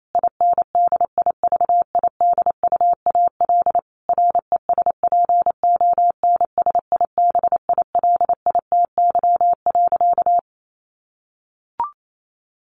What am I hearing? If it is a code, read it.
Morse code, heard as a keyed tone, about 28 wpm, 700 Hz. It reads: INDIVIDUAL RESPONSIBILITY.
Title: Sentences from Top 1500 Words 28wpm